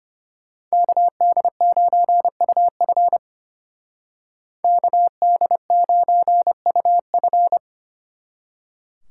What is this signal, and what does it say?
Morse code, heard as a keyed tone, about 33 wpm, 700 Hz. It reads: KD9UF KD9UF